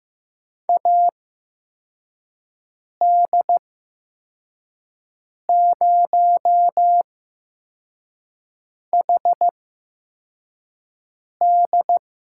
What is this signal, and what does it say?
Morse code, heard as a keyed tone, about 15 wpm, spaced as for 4 wpm, 700 Hz. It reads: AD0HD
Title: USA Call Signs - 5 Characters - 8x Character Spacing - 15wpm